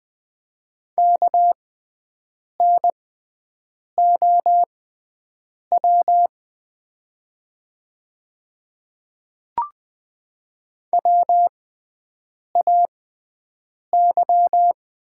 Morse code, heard as a keyed tone, about 20 wpm, spaced as for 7 wpm, 690 Hz. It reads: KNOW WAY
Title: 6x Character Spacing - Top 100 Words - 20wpm